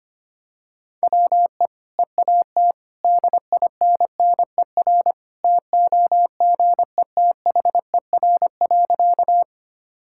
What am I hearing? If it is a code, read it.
Morse code, heard as a keyed tone, about 25 wpm, 690 Hz. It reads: WE EAT DINNER TOGETHER.
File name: Easy Sentences with Hints - 25wpm